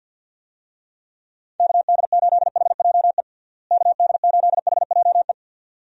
Morse code, heard as a keyed tone, about 49 wpm, 700 Hz. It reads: KD8HJE KD8HJE
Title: USA Call Signs - 6 Characters - Repeated 2x - 50wpm